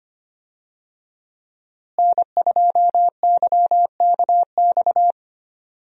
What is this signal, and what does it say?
Morse code, heard as a keyed tone, about 25 wpm, 700 Hz. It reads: N2YKX
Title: USA Call Signs - 5 Characters - Repeated 2x - 25wpm